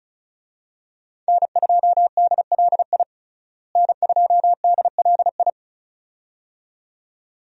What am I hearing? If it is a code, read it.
Morse code, heard as a keyed tone, about 36 wpm, 700 Hz. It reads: N2DLI N2DLI
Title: USA Call Signs - 5 Characters - Repeated 2x - 35wpm